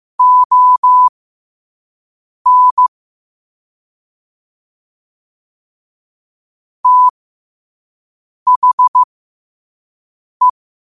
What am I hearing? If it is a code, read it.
Morse code, heard as a keyed tone, about 15 wpm, spaced as for 5 wpm, 1000 Hz. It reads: ON THE
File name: qotdfiles / 5wpm